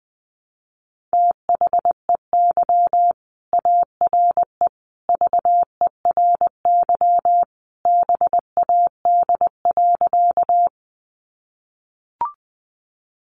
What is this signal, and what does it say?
Morse code, heard as a keyed tone, about 20 wpm, 700 Hz. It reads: THEY ARE VERY BAD.
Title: Sentences from Top 700 Words 20wpm